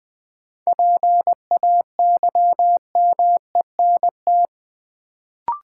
Morse code, heard as a keyed tone, about 20 wpm, 700 Hz. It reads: PAYMENT
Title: Sentences from Top 1200 Words - New Word Drill - 20wpm